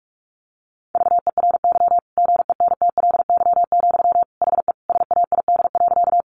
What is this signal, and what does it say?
Morse code, heard as a keyed tone, about 44 wpm, 700 Hz. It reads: VERY GENTLY, HE SAID.
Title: Sentences from Top 1500 Words 45wpm